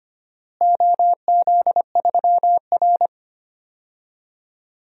Morse code, heard as a keyed tone, about 25 wpm, 700 Hz. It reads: OZ3R
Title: DXCC-100 Call Signs - 4 Characters - Repeated 2x - 25wpm